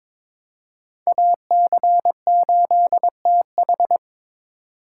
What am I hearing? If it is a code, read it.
Morse code, heard as a keyed tone, about 21 wpm, 700 Hz. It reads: AC8TH
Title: USA Call Signs - 8x Character Spacing - 22wpm